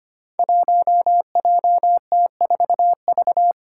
Morse code, heard as a keyed tone, about 25 wpm, 700 Hz. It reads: 1JT4V
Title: Five Challenging Letter-Number Combinations 25wpm